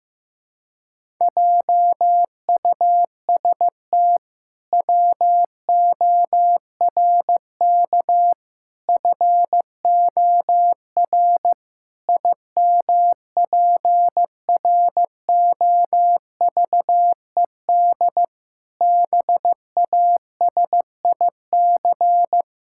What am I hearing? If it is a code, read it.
Morse code, heard as a keyed tone, about 15 wpm, 700 Hz. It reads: JUST WORK FOR IMPROVED BASIC
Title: Pangram Challenge v2 - 15wpm